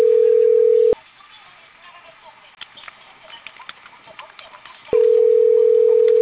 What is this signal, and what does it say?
Ringback tone (North America), 440 + 480 Hz.